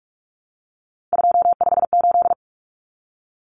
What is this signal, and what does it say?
Morse code, heard as a keyed tone, about 44 wpm, 700 Hz. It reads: 258
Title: Numbers - 3-Digit 45wpm